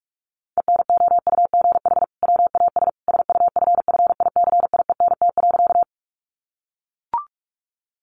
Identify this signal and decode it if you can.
Morse code, heard as a keyed tone, about 44 wpm, 700 Hz. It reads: ENOUGH WAS SUFFICIENT.
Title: Sentences from Top 1400 Words 45wpm